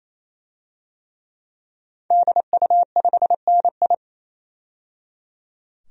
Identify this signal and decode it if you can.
Morse code, heard as a keyed tone, about 27 wpm, 700 Hz. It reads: DU5NI
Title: DXCC-100 Call Signs - 5 Characters - 28wpm